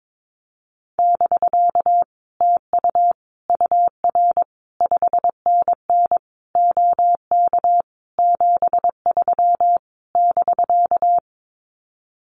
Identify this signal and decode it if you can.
Morse code, heard as a keyed tone, about 22 wpm, 700 Hz. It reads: <BK> TU UR 5NN OK 73 <BK>
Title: Common QSO Phrases Encore 22wpm